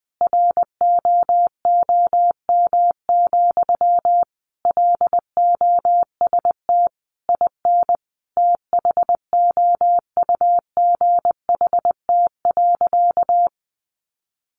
Morse code, heard as a keyed tone, about 20 wpm, 700 Hz. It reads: ROOM, LOST IN THOUGHT.
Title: Sentences from Top 1400 Words 20wpm